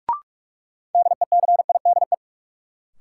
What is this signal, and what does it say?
Morse code, heard as a keyed tone, about 42 wpm, 700 Hz. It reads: DECIDE
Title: Sentences from Top 500 Words - New Word Drill - 45wpm